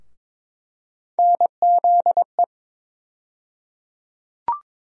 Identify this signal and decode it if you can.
Morse code, heard as a keyed tone, about 21 wpm, 700 Hz. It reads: NZE
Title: USA Call Sign Postfixes - 3 Letters 22wpm